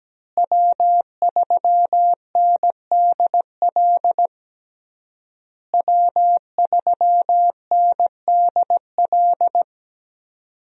Morse code, heard as a keyed tone, about 17 wpm, 700 Hz. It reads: W3NDL W3NDL
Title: USA Call Signs - 5 Characters - Repeated 3x - 17wpm